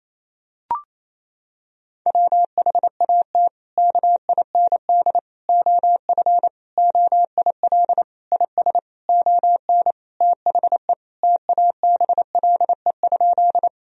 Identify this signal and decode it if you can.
Morse code, heard as a keyed tone, about 28 wpm, 700 Hz. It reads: WHAT KIND OF OIL IS ON THE TABLE?
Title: Sentences from Top 1000 Words - Review Part 4 - 28wpm